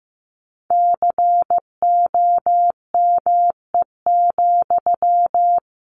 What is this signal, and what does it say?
Morse code, heard as a keyed tone, about 15 wpm, 700 Hz. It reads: COME,
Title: Sentences from Top 1500 Words 15wpm